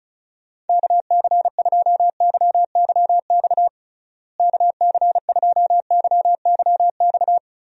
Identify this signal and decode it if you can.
Morse code, heard as a keyed tone, about 36 wpm, 700 Hz. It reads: KC2YYX KC2YYX
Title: USA Call Signs - 6 Characters - Repeated 2x - 35wpm